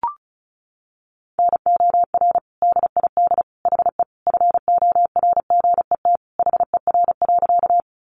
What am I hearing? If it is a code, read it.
Morse code, heard as a keyed tone, about 34 wpm, 700 Hz. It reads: NOR DID HE FORGET HER.
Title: Sentences from Top 1500 Words 35wpm